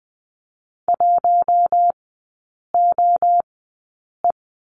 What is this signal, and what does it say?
Morse code, heard as a keyed tone, about 20 wpm, spaced as for 8 wpm, 700 Hz. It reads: 1OE
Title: 047 Lesson 9e - Number 4 - Three Characters 20wpm